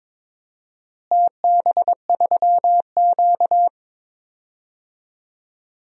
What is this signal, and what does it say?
Morse code, heard as a keyed tone, about 22 wpm, 700 Hz. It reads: TB3Q
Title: DXCC-100 Call Signs - 4 Characters - Repeated 2x - 22wpm